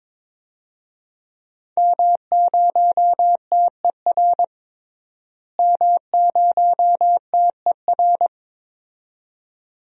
Morse code, heard as a keyed tone, about 22 wpm, 700 Hz. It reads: M0TER M0TER
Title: DXCC-100 Call Signs - 5 Characters - Repeated 2x - 22wpm